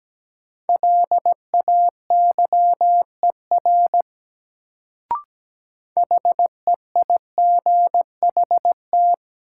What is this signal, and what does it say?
Morse code, heard as a keyed tone, about 17 wpm, 700 Hz. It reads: LAYER HEIGHT
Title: Sentences from Top 1600 Words - New Word Drill - 17wpm